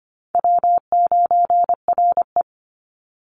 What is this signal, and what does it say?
Morse code, heard as a keyed tone, about 25 wpm, 700 Hz. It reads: W9RE
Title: Easy Call Signs Repeated 2x - 25wpm